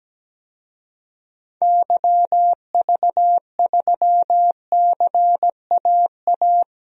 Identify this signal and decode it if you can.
Morse code, heard as a keyed tone, about 17 wpm, 700 Hz. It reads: YV3CAA
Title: DXCC-100 Call Signs - 6 Characters - Repeated 2x - 17wpm